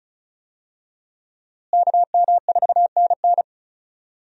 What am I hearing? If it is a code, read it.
Morse code, heard as a keyed tone, about 36 wpm, 700 Hz. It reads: KM4NN
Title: USA Call Signs - 5 Characters - Repeated 2x - 35wpm